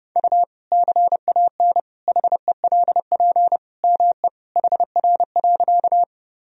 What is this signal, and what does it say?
Morse code, heard as a keyed tone, about 30 wpm, 700 Hz. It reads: U CAN HELP ME HR.
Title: Sentences from Top 200 Words and Abbreviations 30wpm